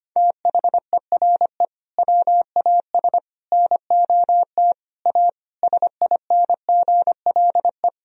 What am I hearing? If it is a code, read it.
Morse code, heard as a keyed tone, about 25 wpm, 700 Hz. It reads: THERE WAS NOT A SINGLE
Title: Sentences from Top 1000 Words - Review Part 4 - 25wpm